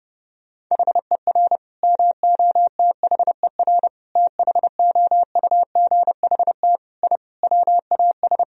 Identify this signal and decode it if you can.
Morse code, heard as a keyed tone, about 30 wpm, 700 Hz. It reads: HER MOTHER THOUGHT I WAS
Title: Sentences from Top 1000 Words - Review Part 4 - 30wpm